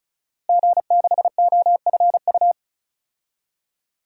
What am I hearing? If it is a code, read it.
Morse code, heard as a keyed tone, about 36 wpm, 700 Hz. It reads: G6OFU
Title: DXCC-100 Call Signs - 5 Characters - Repeated 2x - 35wpm